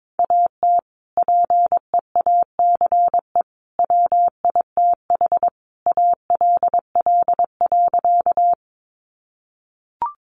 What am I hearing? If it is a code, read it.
Morse code, heard as a keyed tone, about 22 wpm, 700 Hz. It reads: AT PEACE WITH ALL.
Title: Sentences from Top 1500 Words 22wpm